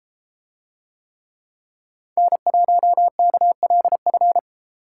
Morse code, heard as a keyed tone, about 30 wpm, 700 Hz. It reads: N1KLF